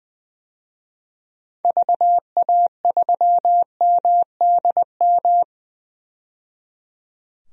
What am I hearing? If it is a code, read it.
Morse code, heard as a keyed tone, about 20 wpm, 700 Hz. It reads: VA3MDM
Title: DXCC-100 Call Signs - 6 Characters - 20wpm